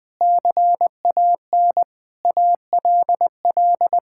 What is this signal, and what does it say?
Morse code, heard as a keyed tone, about 20 wpm, 700 Hz. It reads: CAN ALL
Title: Sentences with 7-Word Phrases - 20wpm